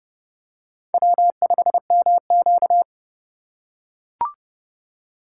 Morse code, heard as a keyed tone, about 28 wpm, 700 Hz. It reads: W5MQ